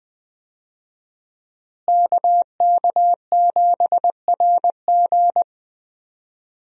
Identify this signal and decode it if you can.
Morse code, heard as a keyed tone, about 20 wpm, 700 Hz. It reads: KK7RG
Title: USA Call Signs - 5 Characters - Repeated 3x - 20wpm